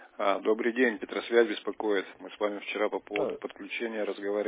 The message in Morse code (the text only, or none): none